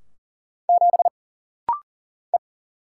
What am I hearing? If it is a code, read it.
Morse code, heard as a keyed tone, about 40 wpm, 700 Hz. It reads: 7 E
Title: Single Letter-Number - Rapid-Fire 40wpm